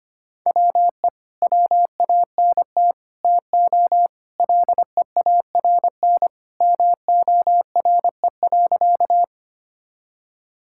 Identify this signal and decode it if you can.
Morse code, heard as a keyed tone, about 25 wpm, 700 Hz. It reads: WE WANT TO LEARN MORE.
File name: practice / 25wpm / Sentences with 3-Word Phrases - 25wpm